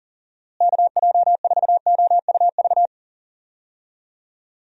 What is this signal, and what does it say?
Morse code, heard as a keyed tone, about 40 wpm, 700 Hz. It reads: KJ4OUV
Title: USA Call Signs - 6 Characters - Repeated 3x - 40wpm